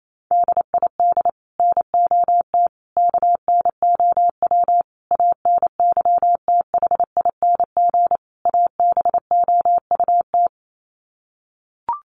Morse code, heard as a keyed tone, about 28 wpm, 700 Hz. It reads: DID NOT KNOW ANYTHING ABOUT
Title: Sets of 5 Words Encore2 28wpm